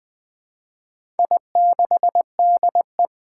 Morse code, heard as a keyed tone, about 20 wpm, 700 Hz. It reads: I6DE
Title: Four Challenging Letter-Number Combinations 20wpm